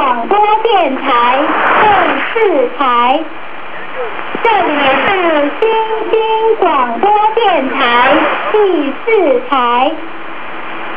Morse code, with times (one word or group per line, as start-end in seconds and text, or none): none